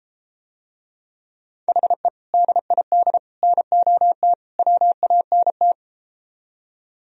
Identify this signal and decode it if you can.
Morse code, heard as a keyed tone, about 34 wpm, 700 Hz. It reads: HE DID NOT WANT